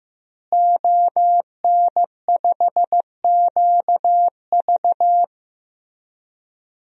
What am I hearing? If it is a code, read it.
Morse code, heard as a keyed tone, about 15 wpm, 700 Hz. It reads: ON5QV
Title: DXCC-100 Call Signs - 5 Characters - Repeated 2x - 15wpm